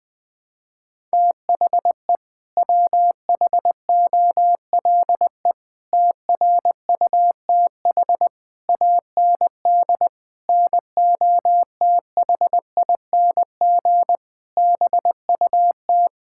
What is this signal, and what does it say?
Morse code, heard as a keyed tone, about 20 wpm, 700 Hz. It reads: THE WHOLE TRUTH AND NOTHING BUT
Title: Sets of 8 Words - 20wpm